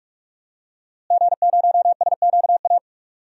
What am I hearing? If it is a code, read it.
Morse code, heard as a keyed tone, about 42 wpm, 700 Hz. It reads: G0SQA
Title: DXCC-100 Call Signs - 5 Characters - Repeated 2x - 45wpm